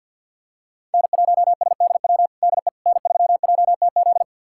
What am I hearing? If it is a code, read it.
Morse code, heard as a keyed tone, about 52 wpm, 700 Hz. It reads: N1SDW DE N3JTZ
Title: USA Callsign DE Callsign - 5 Characters - 50wpm